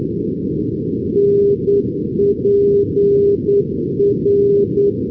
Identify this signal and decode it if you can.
Morse code, heard as a keyed tone, about 9 wpm, 400 Hz. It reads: NPR
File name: europe / pol / pol-415 np pol npr-488-2015